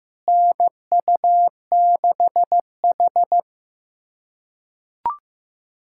Morse code, heard as a keyed tone, about 15 wpm, 700 Hz. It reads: NU6H
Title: USA Callsign DE Callsign - 4 Characters - 15wpm